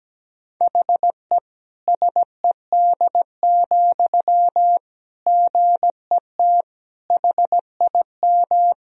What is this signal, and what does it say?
Morse code, heard as a keyed tone, about 17 wpm, 700 Hz. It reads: HE SED, GET HIM
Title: Sentences from Top 400 Words and Abbreviations 17wpm